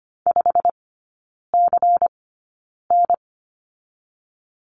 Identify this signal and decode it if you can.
Morse code, heard as a keyed tone, about 25 wpm, spaced as for 9 wpm, 700 Hz. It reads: 5CN
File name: course / 25:10wpm / 093 Lesson 16e - Letter c - Three Characters 25wpm